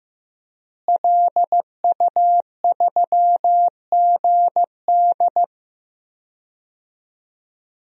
Morse code, heard as a keyed tone, about 15 wpm, 700 Hz. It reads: LU3GD
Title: DXCC-100 Call Signs - 5 Characters - Repeated 2x - 15wpm